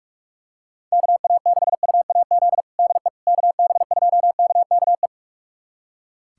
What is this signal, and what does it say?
Morse code, heard as a keyed tone, about 44 wpm, 700 Hz. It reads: KA6UAZ DE KB2KKE
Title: USA Callsign DE Callsign - 6 Characters - 45wpm